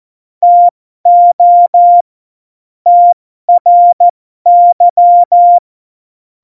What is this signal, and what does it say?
Morse code, heard as a keyed tone, about 14 wpm, spaced as for 12 wpm, 700 Hz. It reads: TO TRY
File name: Morsekurs / Witze